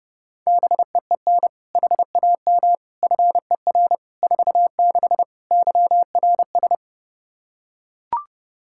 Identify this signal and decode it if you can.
Morse code, heard as a keyed tone, about 30 wpm, 700 Hz. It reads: BEEN HAM FER 46 YRS
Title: Practice QSO - Exchange 3 - More Info - 30wpm